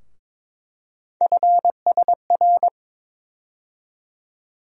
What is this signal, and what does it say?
Morse code, heard as a keyed tone, about 22 wpm, 700 Hz. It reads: FSR
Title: USA Call Sign Postfixes - 3 Letters 22wpm